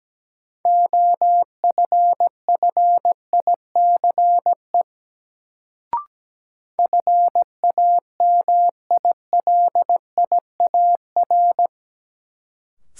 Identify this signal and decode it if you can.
Morse code, heard as a keyed tone, about 17 wpm, 700 Hz. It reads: OFFICE FAMILIAR
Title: Sentences from Top 1000 Words - New Word Drill - 17wpm